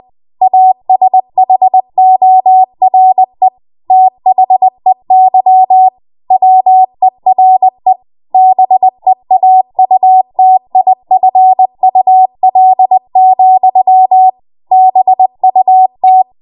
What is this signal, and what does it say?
Morse code, heard as a keyed tone, about 20 wpm, 750 Hz. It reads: ASHORE THEY WERE BEAUTIFUL, BUT
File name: Morse / 20WPM